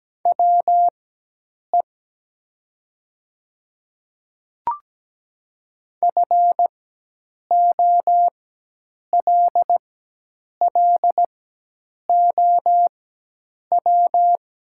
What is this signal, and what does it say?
Morse code, heard as a keyed tone, about 17 wpm, spaced as for 8 wpm, 700 Hz. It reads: WE FOLLOW
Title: 4x Character Spacing - Top 200 Words - 17wpm